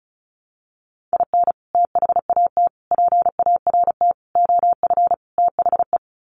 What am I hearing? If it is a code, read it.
Morse code, heard as a keyed tone, about 34 wpm, 700 Hz. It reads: IN THAT PART OF THE
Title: Sets of 5 Words Encore2 35wpm